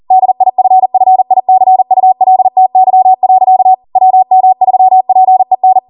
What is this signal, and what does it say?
Morse code, heard as a keyed tone, about 40 wpm, 750 Hz. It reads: DIFFICULTY. WM3PEN